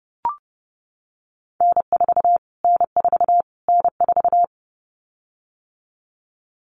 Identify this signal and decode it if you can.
Morse code, heard as a keyed tone, about 28 wpm, 700 Hz. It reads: N4 N4 N4
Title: Callsign Prefixes for Top 25 DXCC - Repeated 3x - 30wpm